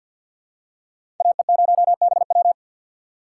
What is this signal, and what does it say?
Morse code, heard as a keyed tone, about 52 wpm, 700 Hz. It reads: AE0BW
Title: USA Call Signs - 2x Character Spacing - 50wpm